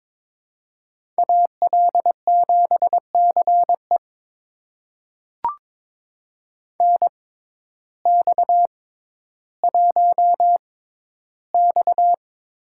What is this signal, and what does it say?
Morse code, heard as a keyed tone, about 22 wpm, spaced as for 12 wpm, 700 Hz. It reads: AL7CE NX1X
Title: USA Call Signs - 6x Character Spacing - 22wpm